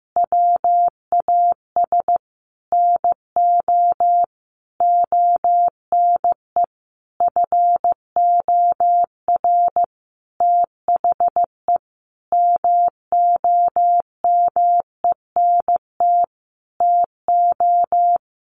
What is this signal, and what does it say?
Morse code, heard as a keyed tone, about 15 wpm, 700 Hz. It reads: WAS NO ONE FOR THE MOMENT TO
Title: Sentences from Top 1500 Words 15wpm